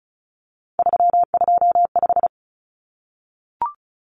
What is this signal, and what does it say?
Morse code, heard as a keyed tone, about 34 wpm, 700 Hz. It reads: 325
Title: Numbers - 3-Digit 35wpm